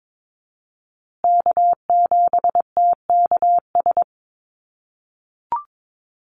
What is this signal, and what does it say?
Morse code, heard as a keyed tone, about 22 wpm, 700 Hz. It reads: K7TKS
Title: USA Call Signs Repeated 3x 22wpm